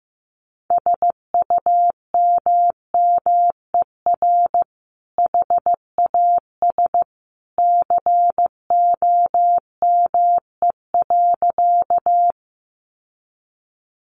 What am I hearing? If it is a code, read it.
Morse code, heard as a keyed tone, about 15 wpm, 700 Hz. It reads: SUMMER HAS COME.
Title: Sentences from Top 1100 Words 15wpm